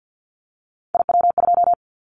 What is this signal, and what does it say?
Morse code, heard as a keyed tone, about 50 wpm, 700 Hz. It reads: IW2
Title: Callsign Prefixes for Top 25 DXCC - 50wpm